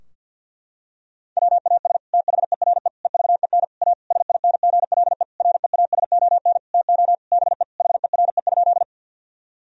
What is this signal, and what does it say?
Morse code, heard as a keyed tone, about 50 wpm, 700 Hz. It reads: WAS THERE EVEN A SINGLE REASON TO BE HERE?
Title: Sentences from Top 1000 Words - Review Part 4 - 50wpm